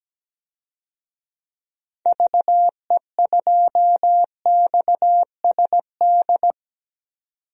Morse code, heard as a keyed tone, about 17 wpm, 700 Hz. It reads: VE2XSD
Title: DXCC-100 Call Signs - 6 Characters - 17wpm